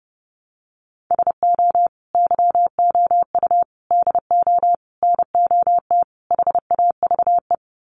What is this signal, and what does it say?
Morse code, heard as a keyed tone, about 30 wpm, 700 Hz. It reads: SO YOU DO NOT HAVE
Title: Sets of 5 Words Encore 30wpm